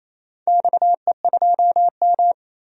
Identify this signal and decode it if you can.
Morse code, heard as a keyed tone, about 28 wpm, 700 Hz. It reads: XE2M
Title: DXCC-100 Call Signs - 4 Characters - 28wpm